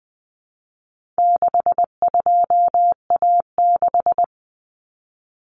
Morse code, heard as a keyed tone, about 20 wpm, 700 Hz. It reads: 62A6
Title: Common Cut Numbers - 4-Digit 20wpm